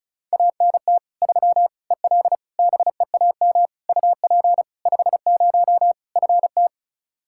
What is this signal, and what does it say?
Morse code, heard as a keyed tone, about 36 wpm, 700 Hz. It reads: ANT 3 EL BEAM UP 50 FT
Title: Practice QSO - Exchange 2 - Info - 35wpm